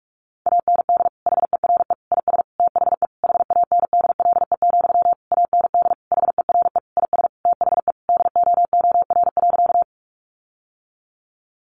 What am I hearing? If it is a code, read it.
Morse code, heard as a keyed tone, about 44 wpm, 700 Hz. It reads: AND HERE IS THE HANDLE, AND HERE IS THE DOOR.
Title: Sentences from Top 1100 Words 45wpm